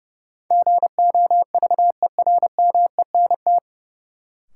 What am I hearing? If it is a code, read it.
Morse code, heard as a keyed tone, about 30 wpm, 700 Hz. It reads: GOVERMENT
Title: Sentences from Top 1300 Words - New Word Drill - 30wpm